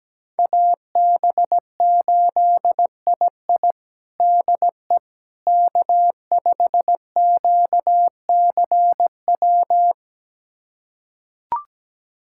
Morse code, heard as a keyed tone, about 17 wpm, 700 Hz. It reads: AB8II DE K5QCW
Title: USA Callsign DE Callsign - 5 Characters - 17wpm